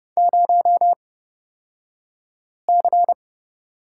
Morse code, heard as a keyed tone, about 30 wpm, spaced as for 5 wpm, 700 Hz. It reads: 0C
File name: Lesson 37d - Letter x - Two Characters 30wpm